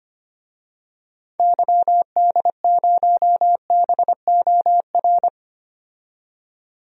Morse code, heard as a keyed tone, about 25 wpm, 700 Hz. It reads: YD0BOR
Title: DXCC-100 Call Signs - 6 Characters - 25wpm